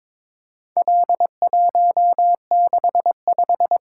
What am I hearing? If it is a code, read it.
Morse code, heard as a keyed tone, about 22 wpm, 700 Hz. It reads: L165
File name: practice / 22wpm / Four Letter-Number Combinations - 22wpm